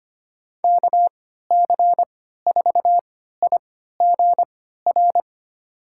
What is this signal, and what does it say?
Morse code, heard as a keyed tone, about 25 wpm, spaced as for 14 wpm, 700 Hz. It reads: KC4IGR
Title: USA Call Signs - 6 Characters - 3x Character Spacing - 25wpm